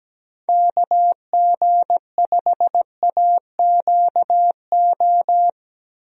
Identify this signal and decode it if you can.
Morse code, heard as a keyed tone, about 17 wpm, 700 Hz. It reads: KG5AQO
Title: USA Call Signs - 6 Characters - Repeated 3x - 17wpm